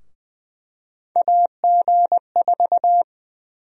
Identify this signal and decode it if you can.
Morse code, heard as a keyed tone, about 21 wpm, 700 Hz. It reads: AG4
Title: USA Call Sign Prefixes - Repeated 2x - 20wpm